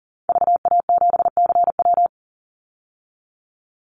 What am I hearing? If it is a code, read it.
Morse code, heard as a keyed tone, about 40 wpm, 700 Hz. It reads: VA7CW
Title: Call Signs Repeated 2x - 40wpm